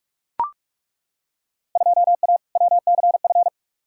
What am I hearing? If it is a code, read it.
Morse code, heard as a keyed tone, about 42 wpm, 700 Hz. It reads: 2A WCF
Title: Field Day Exchanges - 45wpm